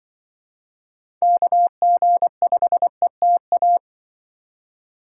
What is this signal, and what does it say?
Morse code, heard as a keyed tone, about 24 wpm, 700 Hz. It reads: KG5ETA
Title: Speed Racing - USA Call Signs 20wpm